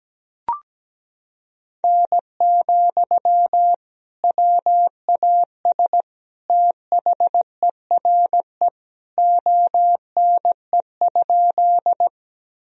Morse code, heard as a keyed tone, about 17 wpm, 700 Hz. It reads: N, WAS THERE ONE?
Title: Sentences from Top 100 Words and Abbreviations 17wpm